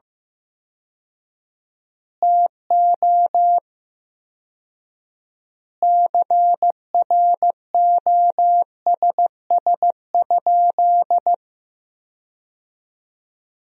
Morse code, heard as a keyed tone, about 15 wpm, 700 Hz. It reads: TO CROSS?
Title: 4x Word-Spacing - Sentences from Top 300 Words - 15wpm